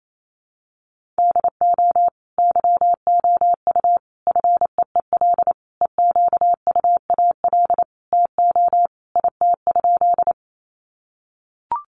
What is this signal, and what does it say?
Morse code, heard as a keyed tone, about 28 wpm, 700 Hz. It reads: DO YOU FEEL EQUAL TO IT?
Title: Sentences from Top 1400 Words 28wpm